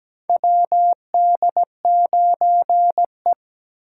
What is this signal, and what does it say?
Morse code, heard as a keyed tone, about 17 wpm, 700 Hz. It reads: WD9E
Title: USA Call Signs - 4 Characters - 3x Character Spacing - 17wpm